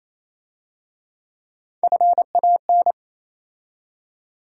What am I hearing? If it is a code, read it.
Morse code, heard as a keyed tone, about 27 wpm, 700 Hz. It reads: FAN
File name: Easy Words - Speed-Racing with Hints - 22wpm